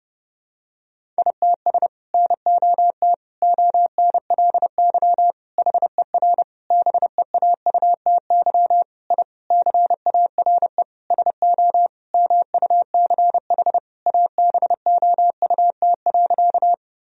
Morse code, heard as a keyed tone, about 30 wpm, 700 Hz. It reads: ITS NOT ONLY HER BEAUTY I CARE SO MUCH ABOUT.
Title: Sentences from Top 1000 Words - Review Part 2 - 30wpm